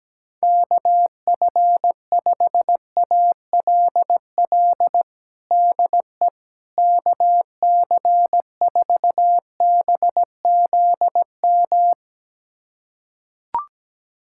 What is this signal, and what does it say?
Morse code, heard as a keyed tone, about 17 wpm, 700 Hz. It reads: KF5ALL DE KC4BZM
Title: USA Callsign DE Callsign - 6 Characters - 17wpm